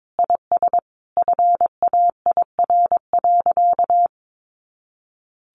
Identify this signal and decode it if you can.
Morse code, heard as a keyed tone, about 22 wpm, 700 Hz. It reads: IS FAIR.
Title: Sentences from Top 1300 Words 22wpm